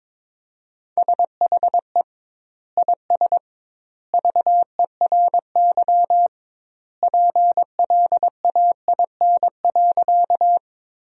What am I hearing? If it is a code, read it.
Morse code, heard as a keyed tone, about 22 wpm, 700 Hz. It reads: SHE IS VERY PLAIN.
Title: 2x Word-Spacing - Sentences from Top 400 Words - 22wpm